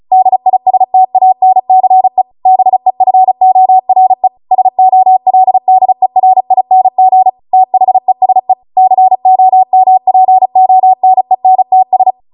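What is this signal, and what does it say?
Morse code, heard as a keyed tone, about 36 wpm, 750 Hz. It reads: DISTANCE BEFORE SOLDERING THESE COMPONENTS
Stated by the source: Amateur radio Morse code practice archive transmission at 35 wpm.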